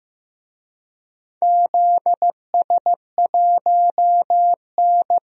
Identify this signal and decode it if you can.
Morse code, heard as a keyed tone, about 15 wpm, 700 Hz. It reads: ZS1N
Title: DXCC-100 Call Signs - 4 Characters - 15wpm